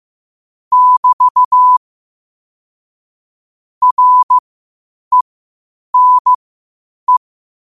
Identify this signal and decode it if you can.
Morse code, heard as a keyed tone, about 15 wpm, spaced as for 8 wpm, 1000 Hz. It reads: <BT> RENE
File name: qotdfiles / 8wpm